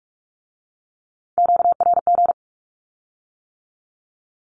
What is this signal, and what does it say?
Morse code, heard as a keyed tone, about 46 wpm, 700 Hz. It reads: QRG
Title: Q-Codes with Spoken Meaning - 45wpm